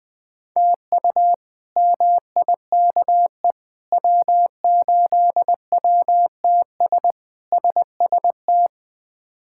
Morse code, heard as a keyed tone, about 21 wpm, 700 Hz. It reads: TU MIKE W8WTS SST
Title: K1USN SST QSO Practice - 20wpm